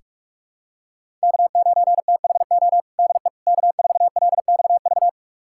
Morse code, heard as a keyed tone, about 46 wpm, 700 Hz. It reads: K9THO DE K4LXV
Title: USA Callsign DE Callsign 45wpm